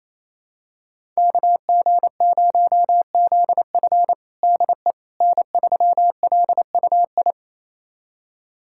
Morse code, heard as a keyed tone, about 28 wpm, 700 Hz. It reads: KG0ZF DE N3LUI
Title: USA Callsign DE Callsign - 5 Characters - 28wpm